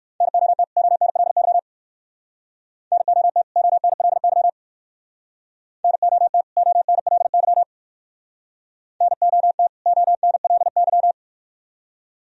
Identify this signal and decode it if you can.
Morse code, heard as a keyed tone, about 59 wpm, 700 Hz. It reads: NOT ONLY NOT ONLY NOT ONLY NOT ONLY